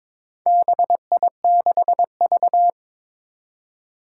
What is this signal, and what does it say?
Morse code, heard as a keyed tone, about 23 wpm, 700 Hz. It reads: BI6V
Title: Four Challenging Letter-Number Combinations 22wpm